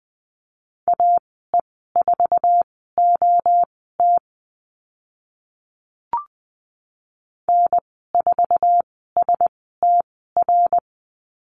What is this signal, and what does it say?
Morse code, heard as a keyed tone, about 20 wpm, spaced as for 14 wpm, 700 Hz. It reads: AE4OT N4STR
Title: 069 Lesson 12g - Letter d - Call Signs 20wpm